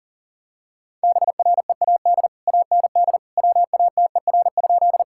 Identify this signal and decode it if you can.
Morse code, heard as a keyed tone, about 40 wpm, 700 Hz. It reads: BREAD AND WATER?
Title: Sentences from Top 1600 Words 40wpm